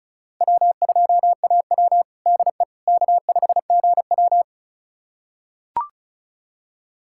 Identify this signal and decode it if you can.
Morse code, heard as a keyed tone, about 36 wpm, 700 Hz. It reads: W2AW DE K5GW
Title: USA Callsign DE Callsign - 4 Characters - 35wpm